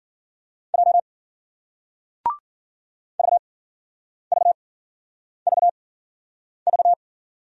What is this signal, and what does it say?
Morse code, heard as a keyed tone, about 56 wpm, spaced as for 9 wpm, 700 Hz. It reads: J VVVV